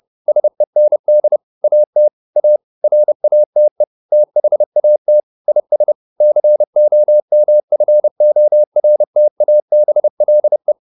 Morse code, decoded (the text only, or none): SEND AT A RATE THAT IS COMFORTABLE